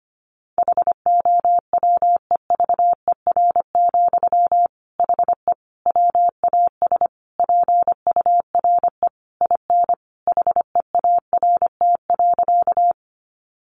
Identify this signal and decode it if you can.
Morse code, heard as a keyed tone, about 25 wpm, 700 Hz. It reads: HOWEVER, HE WAS PURE IN HEART.
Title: Sentences from Top 1500 Words 25wpm